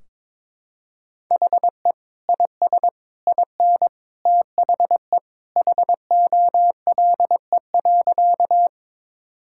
Morse code, heard as a keyed tone, about 22 wpm, 700 Hz. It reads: HE IS IN THE HOLE.
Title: Sentences from Top 1600 Words 22wpm